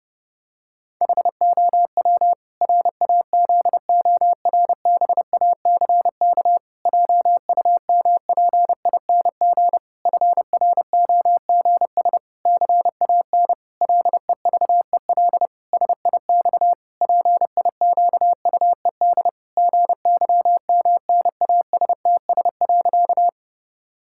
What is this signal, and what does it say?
Morse code, heard as a keyed tone, about 31 wpm, 700 Hz. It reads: HOW RAZORBACK JUMPING FROGS CAN LEVEL SIX PIQUED GYMNASTS.
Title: Pangram Challenge v2 - 30wpm